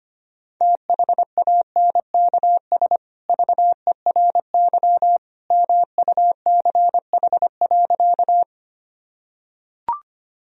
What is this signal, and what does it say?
Morse code, heard as a keyed tone, about 25 wpm, 700 Hz. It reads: THANKS VERY MUCH.
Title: Sentences from Top 1000 Words - Review Part 3 - 25wpm